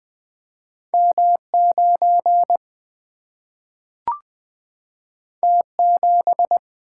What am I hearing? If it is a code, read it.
Morse code, heard as a keyed tone, about 20 wpm, 700 Hz. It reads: M 9 T 7
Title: Single-Letter Single-Number 20wpm